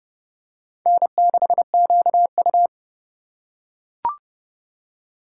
Morse code, heard as a keyed tone, about 29 wpm, 700 Hz. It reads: N6QU
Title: USA Call Signs - 4 Characters - 2x Character Spacing - 30wpm